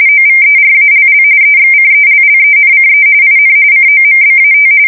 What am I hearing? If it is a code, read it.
RTTY, about 45.45 baud. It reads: OVERALL EXCELLENCE IN COMBAT
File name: fsk170 / USS-New-Jersey